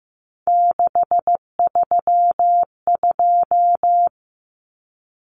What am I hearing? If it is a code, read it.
Morse code, heard as a keyed tone, about 15 wpm, 700 Hz. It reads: 632
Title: Numbers - 3-Digit 15wpm